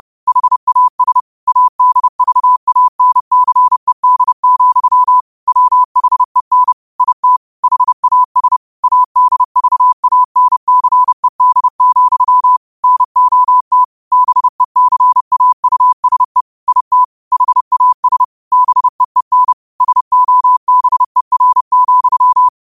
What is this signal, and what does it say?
Morse code, heard as a keyed tone, about 30 wpm, 1000 Hz. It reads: HAS ADVANCED, WHEN IT HAS ADVANCED, NOT BECAUSE IT HAS BEEN SOBER,
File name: qotdfiles / 30wpm